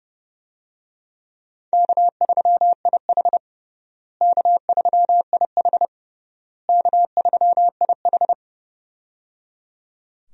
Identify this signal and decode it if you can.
Morse code, heard as a keyed tone, about 30 wpm, 700 Hz. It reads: K3IH K3IH K3IH
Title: USA Call Signs - 4 Characters - Repeated 3x - 30wpm